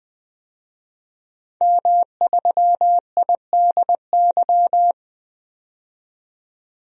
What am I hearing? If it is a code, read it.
Morse code, heard as a keyed tone, about 20 wpm, 700 Hz. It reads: M3IDY
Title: DXCC-100 Call Signs - 5 Characters - 20wpm